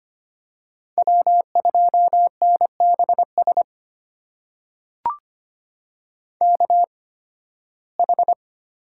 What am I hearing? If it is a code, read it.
Morse code, heard as a keyed tone, about 25 wpm, spaced as for 16 wpm, 700 Hz. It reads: W2NBS KH
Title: USA Call Signs - 5 Characters - 8x Character Spacing - 25wpm